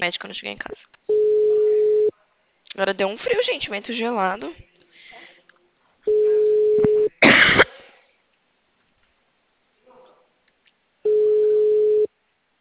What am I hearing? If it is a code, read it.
Ringback tone (Europe, 425 Hz), with speech.